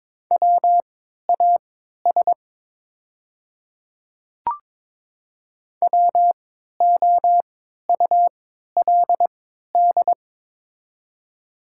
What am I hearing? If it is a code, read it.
Morse code, heard as a keyed tone, about 22 wpm, spaced as for 13 wpm, 700 Hz. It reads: WAS WOULD
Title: 3x Character Spacing - Top 100 Words - 22wpm